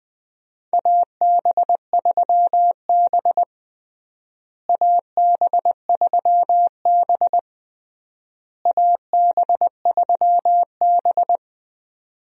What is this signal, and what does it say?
Morse code, heard as a keyed tone, about 21 wpm, 700 Hz. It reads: AB3B AB3B AB3B
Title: USA Call Signs - 4 Characters - Repeated 3x - 20wpm